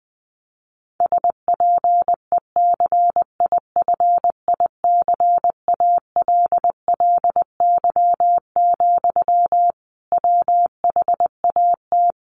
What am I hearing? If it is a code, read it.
Morse code, heard as a keyed tone, about 20 wpm, 700 Hz. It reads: SPECIFICALLY, WHAT
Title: Sentences from Top 1100 Words 20wpm